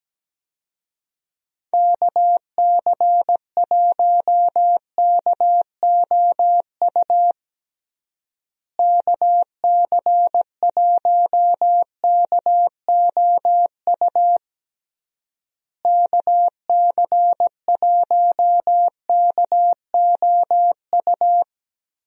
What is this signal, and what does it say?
Morse code, heard as a keyed tone, about 17 wpm, 700 Hz. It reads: KC1KOU KC1KOU KC1KOU
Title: USA Call Signs - 6 Characters - Repeated 3x - 17wpm